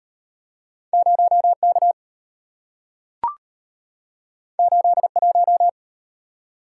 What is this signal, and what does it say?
Morse code, heard as a keyed tone, about 34 wpm, 700 Hz. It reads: 0K 81